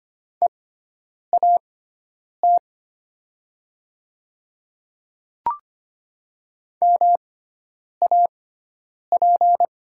Morse code, heard as a keyed tone, about 25 wpm, spaced as for 9 wpm, 700 Hz. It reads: EAT MAP
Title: 6x Character Spacing - Top 500 Words - 25wpm